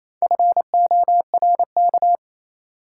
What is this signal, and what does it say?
Morse code, heard as a keyed tone, about 27 wpm, 700 Hz. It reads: FORK
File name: Easy Words - Speed-Racing with Hints - 22wpm